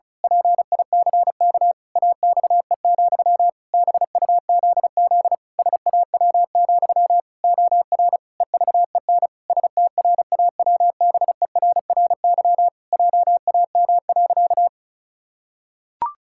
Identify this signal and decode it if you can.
Morse code, heard as a keyed tone, about 36 wpm, 700 Hz. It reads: PICK AXE, BUZZ SAW, OR EVEN STRAWBERRY JAM.
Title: Pangram Challenge v2 - 35wpm